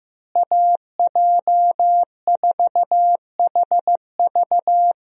Morse code, heard as a keyed tone, about 15 wpm, 700 Hz. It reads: AJ4HV
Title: USA Call Signs - 5 Characters - Repeated 3x - 15wpm